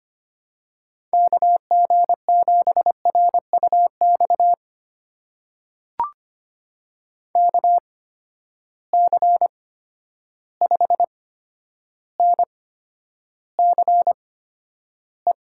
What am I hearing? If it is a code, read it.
Morse code, heard as a keyed tone, about 25 wpm, spaced as for 11 wpm, 700 Hz. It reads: KG7RUX KC5NCE
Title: USA Call Signs - 6 Characters - 8x Character Spacing - 25wpm